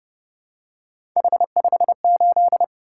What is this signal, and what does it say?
Morse code, heard as a keyed tone, about 30 wpm, 700 Hz. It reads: H58